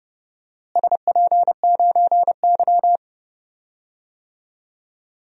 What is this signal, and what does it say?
Morse code, heard as a keyed tone, about 30 wpm, 700 Hz. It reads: SP9Y
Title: DXCC-100 Call Signs - 4 Characters - Repeated 2x - 30wpm